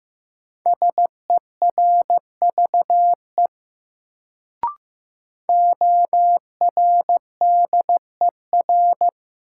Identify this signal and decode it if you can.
Morse code, heard as a keyed tone, about 15 wpm, 700 Hz. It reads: SERVE ORDER
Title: Sentences from Top 400 Words - New Word Drill - 15wpm